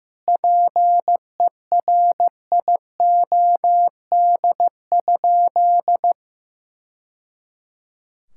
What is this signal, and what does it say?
Morse code, heard as a keyed tone, about 15 wpm, 700 Hz. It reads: PERIOD?
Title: Sentences from Top 1000 Words - Review Part 5 - 15wpm